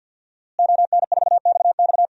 Morse code, heard as a keyed tone, about 52 wpm, 700 Hz. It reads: ON4XX
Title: DXCC-100 Call Signs - 5 Characters - Repeated 2x - 50wpm